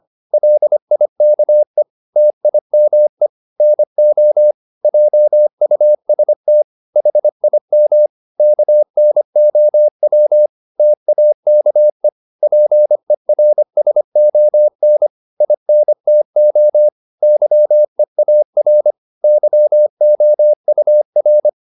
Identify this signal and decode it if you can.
Morse code, heard as a keyed tone, about 25 wpm, 600 Hz. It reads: LIKE TIME NO JUST HIM KNOW TAKE PERSON INTO YEAR YOUR
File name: Top-100-Words-25